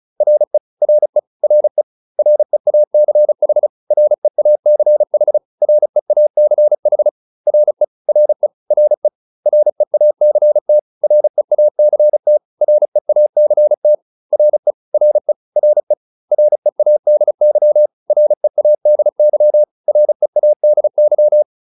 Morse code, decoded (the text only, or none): RE RE RE REACH REACH REACH RE RE RE REACT REACT REACT RE RE RE READY READY READY